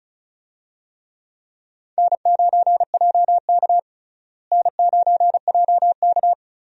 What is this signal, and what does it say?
Morse code, heard as a keyed tone, about 34 wpm, 700 Hz. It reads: N9JK N9JK
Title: USA Call Signs - 4 Characters - Repeated 2x - 35wpm